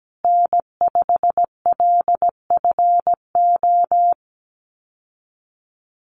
Morse code, heard as a keyed tone, about 17 wpm, 700 Hz. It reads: N5LFO
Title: USA Call Signs Repeated 3x 17wpm